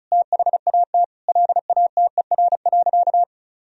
Morse code, heard as a keyed tone, about 34 wpm, 700 Hz. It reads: THAT LATER.
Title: Sentences from Top 1000 Words - Review Part 4 - 35wpm